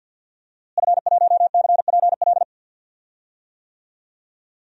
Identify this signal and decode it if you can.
Morse code, heard as a keyed tone, about 52 wpm, 700 Hz. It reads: F1CPL
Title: DXCC-100 Call Signs - 5 Characters - Repeated 2x - 50wpm